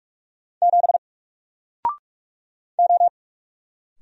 Morse code, heard as a keyed tone, about 44 wpm, 700 Hz. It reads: 7 O
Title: Single Character (Letter:Number:Punctuation:Prosign) 45wpm